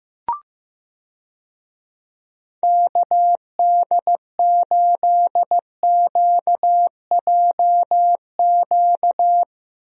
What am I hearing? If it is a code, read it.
Morse code, heard as a keyed tone, about 15 wpm, 700 Hz. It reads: KD8QJQ
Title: USA Call Signs - 6 Characters - Repeated 2x - 15wpm